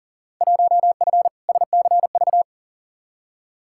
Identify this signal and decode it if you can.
Morse code, heard as a keyed tone, about 37 wpm, 700 Hz. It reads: 1F SCV
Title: Field Day Exchanges Encore - 40wpm